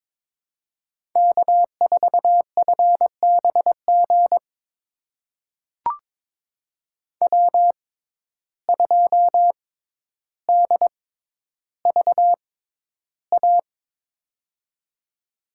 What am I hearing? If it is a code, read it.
Morse code, heard as a keyed tone, about 22 wpm, spaced as for 11 wpm, 700 Hz. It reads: K4FBG W2DVA
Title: USA Call Signs - 5 Characters - 6x Character Spacing - 22wpm